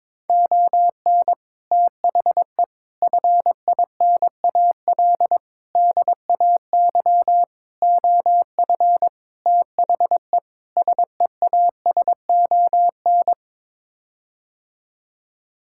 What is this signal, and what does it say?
Morse code, heard as a keyed tone, about 22 wpm, 700 Hz. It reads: ON THE FINAL DAY OF THE SEASON
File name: practice / 22wpm / Sets of 7 Words - 22wpm